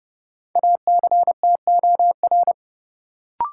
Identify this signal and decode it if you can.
Morse code, heard as a keyed tone, about 30 wpm, 700 Hz. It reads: ACTOR
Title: Sentences from Top 1500 Words - New Word Drill Alt - 30wpm